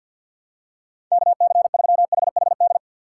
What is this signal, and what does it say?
Morse code, heard as a keyed tone, about 53 wpm, 700 Hz. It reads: KC3HHD
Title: USA Call Signs - 6 Characters - 8x Character Spacing - 50wpm